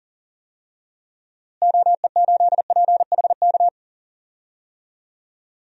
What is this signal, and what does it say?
Morse code, heard as a keyed tone, about 40 wpm, 700 Hz. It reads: OE8PHK
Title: DXCC-100 Call Signs - 6 Characters - Repeated 3x - 40wpm